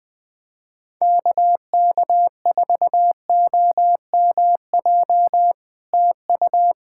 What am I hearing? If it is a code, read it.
Morse code, heard as a keyed tone, about 18 wpm, 700 Hz. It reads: KK4OMJ TU
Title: POTA QSO Practice - 20wpm